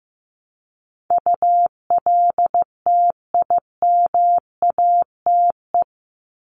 Morse code, heard as a keyed tone, about 15 wpm, 700 Hz. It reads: ULTIMATE
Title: LY Suffix - Top 100 Words with Base Words 15wpm